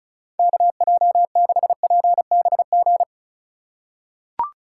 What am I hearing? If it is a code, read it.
Morse code, heard as a keyed tone, about 34 wpm, 700 Hz. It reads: KJ6PBG
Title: USA Call Signs - 2x Character Spacing - 35wpm